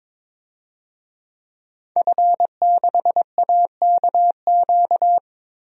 Morse code, heard as a keyed tone, about 22 wpm, 700 Hz. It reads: F6AKQ
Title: DXCC-100 Call Signs - 5 Characters - Repeated 3x - 22wpm